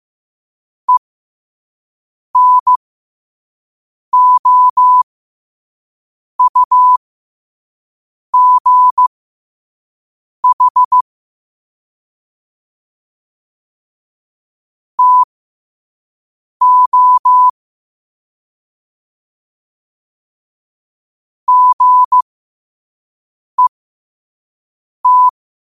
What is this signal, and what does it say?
Morse code, heard as a keyed tone, about 15 wpm, spaced as for 5 wpm, 1000 Hz. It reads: ENOUGH TO GET
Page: New quotes every day in morse code at 5 Words per minute.